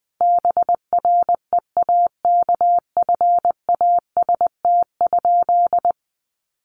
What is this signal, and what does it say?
Morse code, heard as a keyed tone, about 20 wpm, 700 Hz. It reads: BREAKFAST?
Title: Sentences from Top 1500 Words 20wpm